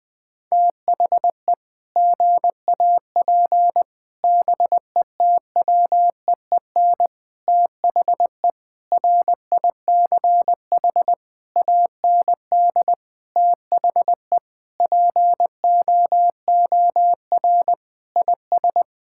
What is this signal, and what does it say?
Morse code, heard as a keyed tone, about 20 wpm, 700 Hz. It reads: THE GAP BETWEEN THE RICH AND THE POOR IS
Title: Sentences with 8-Word Phrases - Spoken Beginning - 20wpm